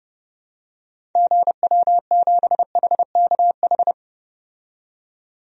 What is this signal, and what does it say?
Morse code, heard as a keyed tone, about 30 wpm, 700 Hz. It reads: GW7HKH
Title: DXCC-100 Call Signs - 6 Characters - 30wpm